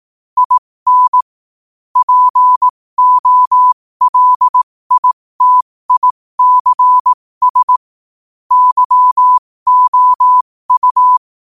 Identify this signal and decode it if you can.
Morse code, heard as a keyed tone, about 18 wpm, spaced as for 15 wpm, 1000 Hz. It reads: IN POLITICS YOU
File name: qotdfiles / 15wpm